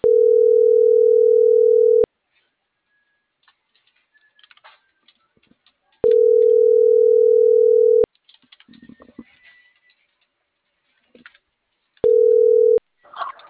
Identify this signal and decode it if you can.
Ringback tone (North America), 440 + 480 Hz.
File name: CENTRAL NMP / out melanie.diaznmp - .40124